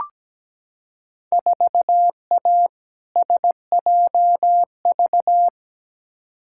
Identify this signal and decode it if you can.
Morse code, heard as a keyed tone, about 17 wpm, 700 Hz. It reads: 4A SJV
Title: Field Day Exchanges Encore - 17wpm